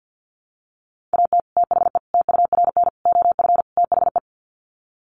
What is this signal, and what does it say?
Morse code, heard as a keyed tone, about 50 wpm, 700 Hz. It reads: AT THE TURN OF THE
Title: Sets of 5 Words 50wpm